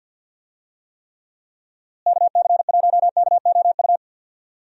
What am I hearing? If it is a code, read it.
Morse code, heard as a keyed tone, about 50 wpm, 700 Hz. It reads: KC1KOU
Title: USA Call Signs - 6 Characters - Repeated 3x - 50wpm